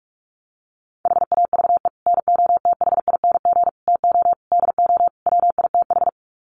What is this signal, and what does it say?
Morse code, heard as a keyed tone, about 44 wpm, 700 Hz. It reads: HAVE NOTHING TO DO WITH
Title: Sets of 5 Words 45wpm